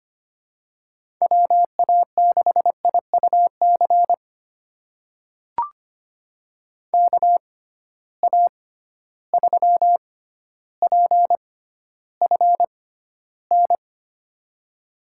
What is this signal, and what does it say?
Morse code, heard as a keyed tone, about 25 wpm, spaced as for 13 wpm, 700 Hz. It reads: WA6IUC KA3PFN
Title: USA Call Signs - 6 Characters - 6x Character Spacing - 25wpm